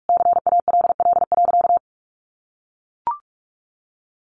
Morse code, heard as a keyed tone, about 44 wpm, 700 Hz. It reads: CALL.
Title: Sentences from Top 1300 Words 45wpm